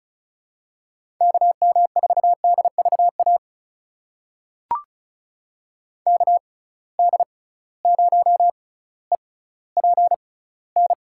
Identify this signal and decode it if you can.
Morse code, heard as a keyed tone, about 34 wpm, spaced as for 18 wpm, 700 Hz. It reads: KM4DVA KD0EPN
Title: USA Call Signs - 6 Characters - 6x Character Spacing - 35wpm